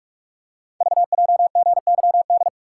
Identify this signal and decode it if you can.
Morse code, heard as a keyed tone, about 44 wpm, 700 Hz. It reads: VJGYD
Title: Code Groups - 5 Letters - 45wpm